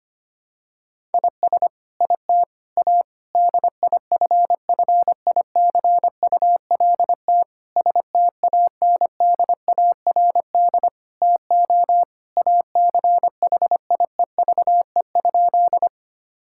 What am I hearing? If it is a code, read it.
Morse code, heard as a keyed tone, about 25 wpm, 700 Hz. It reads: IS IT A DIFFICULT STANDARD TO ACHIEVE?
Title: Sentences from Top 1000 Words - Review Part 6 - 25wpm